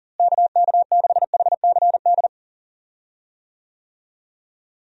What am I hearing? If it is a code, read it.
Morse code, heard as a keyed tone, about 40 wpm, 700 Hz. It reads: KK6HCD
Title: USA Call Signs - 6 Characters - Repeated 3x - 40wpm